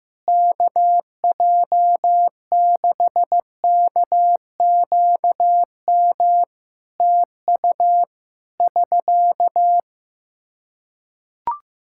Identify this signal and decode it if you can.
Morse code, heard as a keyed tone, about 15 wpm, 700 Hz. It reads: KJ6KQM TU <SK>
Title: Practice QSO - Exchange 4 - Ending - 15wpm